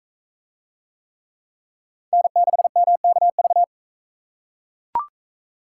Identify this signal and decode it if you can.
Morse code, heard as a keyed tone, about 42 wpm, 700 Hz. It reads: N6MKV